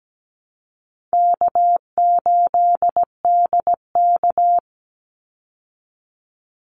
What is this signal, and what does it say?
Morse code, heard as a keyed tone, about 17 wpm, 700 Hz. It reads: K8DK
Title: Easy Call Signs Repeated 2x - 17wpm